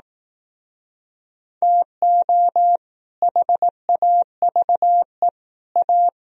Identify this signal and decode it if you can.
Morse code, heard as a keyed tone, about 19 wpm, 700 Hz. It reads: TO HAVE A
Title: Speed-Racing - Sets of 3 Words 15wpm